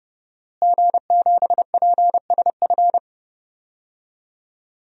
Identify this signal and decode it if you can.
Morse code, heard as a keyed tone, about 31 wpm, 700 Hz. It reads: G7PSF
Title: DXCC-100 Call Signs - 5 Characters - Repeated 3x - 30wpm